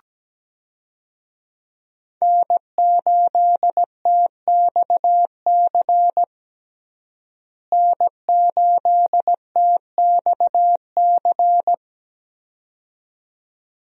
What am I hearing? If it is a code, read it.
Morse code, heard as a keyed tone, about 17 wpm, 700 Hz. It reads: N8TXC N8TXC
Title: USA Call Signs - 5 Characters - Repeated 2x - 17wpm